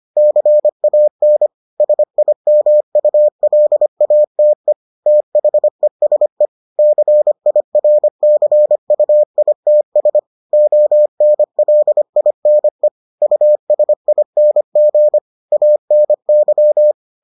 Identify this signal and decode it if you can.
Morse code, heard as a keyed tone, about 25 wpm, 600 Hz. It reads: CAN SIMULATE THESE CIRCUITS ONLINE USING ANY